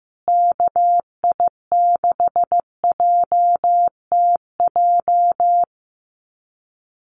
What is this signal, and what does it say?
Morse code, heard as a keyed tone, about 14 wpm, 700 Hz. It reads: KI6JTJ
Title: USA Call Signs Repeated 2x 15wpm